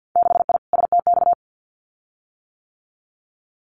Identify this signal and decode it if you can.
Morse code, heard as a keyed tone, about 50 wpm, 700 Hz. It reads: 6I STX
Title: Winter Field Day Exchanges Encore 50wpm